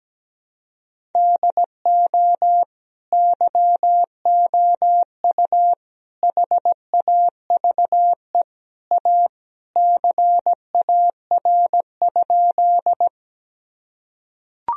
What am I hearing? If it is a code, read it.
Morse code, heard as a keyed tone, about 17 wpm, 700 Hz. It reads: DO YOU HAVE A CAR?
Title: Sentences from Top 1000 Words - Review Part 1 - 17wpm